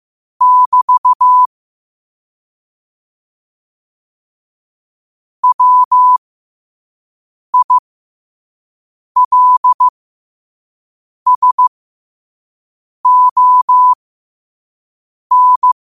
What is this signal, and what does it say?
Morse code, heard as a keyed tone, about 15 wpm, spaced as for 5 wpm, 1000 Hz. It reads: <BT> WILSON